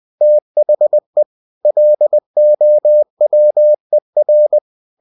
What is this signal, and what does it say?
Morse code, heard as a keyed tone, about 20 wpm, 600 Hz. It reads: THE LOWER
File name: cw-books / winnie-the-pooh / winnie-20wpm-page-0087